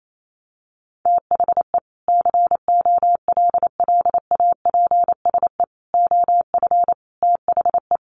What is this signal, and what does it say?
Morse code, heard as a keyed tone, about 28 wpm, 700 Hz. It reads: THE COLLAPSE OF THE
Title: Sets of 5 Words Encore 28wpm